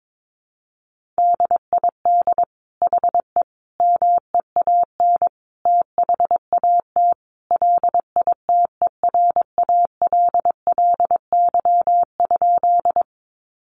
Morse code, heard as a keyed tone, about 22 wpm, 700 Hz. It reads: DID HE MEAN THAT LITERALLY?
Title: Sentences from Top 1400 Words 22wpm